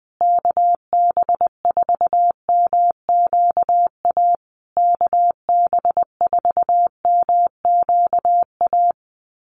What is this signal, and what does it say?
Morse code, heard as a keyed tone, about 19 wpm, 700 Hz. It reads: KB4MQA KB4MQA
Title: USA Call Signs Repeated 3x 20wpm